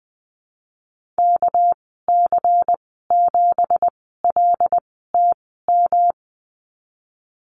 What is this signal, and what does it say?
Morse code, heard as a keyed tone, about 20 wpm, spaced as for 15 wpm, 700 Hz. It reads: KC7LTM
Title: 213 Lesson 34g - Number 8 - Call Signs 20wpm